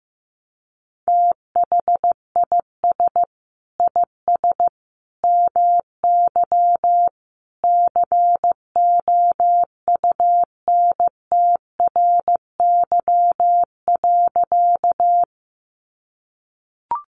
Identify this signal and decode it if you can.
Morse code, heard as a keyed tone, about 15 wpm, 700 Hz. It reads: THIS IS MY COUNTRY.
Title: Sentences from Top 300 Words 15wpm